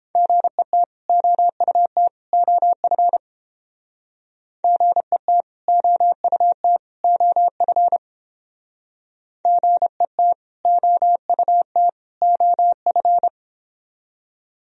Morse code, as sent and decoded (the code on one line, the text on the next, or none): --. - / --- ..- - / --- ..-. / --. - / --- ..- - / --- ..-. / --. - / --- ..- - / --- ..-.
GET OUT OF GET OUT OF GET OUT OF